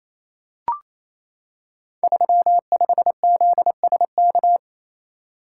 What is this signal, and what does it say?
Morse code, heard as a keyed tone, about 28 wpm, 700 Hz. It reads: 35ZSK
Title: Code Groups - 5 Letter:Numbers - 28wpm